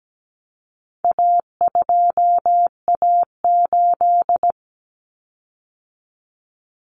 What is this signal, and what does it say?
Morse code, heard as a keyed tone, about 17 wpm, 700 Hz. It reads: A2A8
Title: Common Cut Numbers - 4-Digit 17wpm